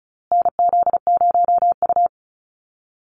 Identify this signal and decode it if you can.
Morse code, heard as a keyed tone, about 34 wpm, 700 Hz. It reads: NZ0U
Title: USA Call Signs Repeated 3x 35wpm